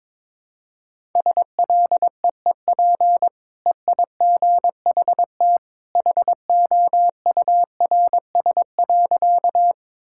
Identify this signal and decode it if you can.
Morse code, heard as a keyed tone, about 22 wpm, 700 Hz. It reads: SLEEP EIGHT HOURS.
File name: practice / 22wpm / Easy Sentences with Hints - 22wpm